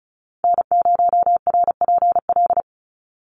Morse code, heard as a keyed tone, about 34 wpm, 700 Hz. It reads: N0RPL
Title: USA Call Signs 35wpm